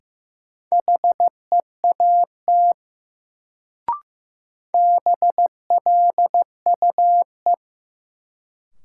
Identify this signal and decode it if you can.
Morse code, heard as a keyed tone, about 16 wpm, 700 Hz. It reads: HEAT BLUE
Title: Sentences from Top 1000 Words - Review Part 3 - New Word Drill - 15wpm